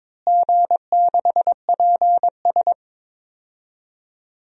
Morse code, heard as a keyed tone, about 21 wpm, 700 Hz. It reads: G6PS
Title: DXCC-100 Call Signs - 4 Characters - Repeated 2x - 22wpm